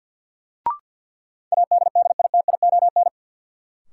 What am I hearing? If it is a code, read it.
Morse code, heard as a keyed tone, about 53 wpm, 700 Hz. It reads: ADDITION
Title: Sentences from Top 1000 Words - Review Part 6 - New Word Drill - 50wpm